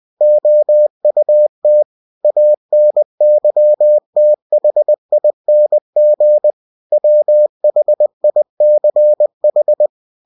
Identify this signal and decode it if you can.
Morse code, heard as a keyed tone, about 20 wpm, 600 Hz. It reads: OUT ANYTHING WHICH
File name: cw-books / winnie-the-pooh / winnie-20wpm-page-0109